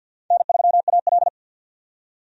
Morse code, heard as a keyed tone, about 52 wpm, 700 Hz. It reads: N3AL
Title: USA Call Signs - 4 Characters - 50wpm